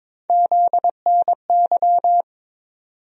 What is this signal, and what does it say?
Morse code, heard as a keyed tone, about 22 wpm, 700 Hz. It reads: ZNY
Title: USA Call Sign Postfixes - 3 Letters 22wpm